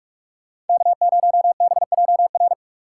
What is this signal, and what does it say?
Morse code, heard as a keyed tone, about 44 wpm, 700 Hz. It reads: K0BJR
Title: USA Call Signs - 6x Character Spacing - 45wpm